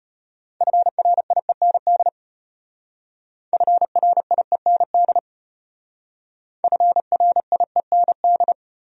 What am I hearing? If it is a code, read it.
Morse code, heard as a keyed tone, about 32 wpm, 700 Hz. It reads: FRIEND FRIEND FRIEND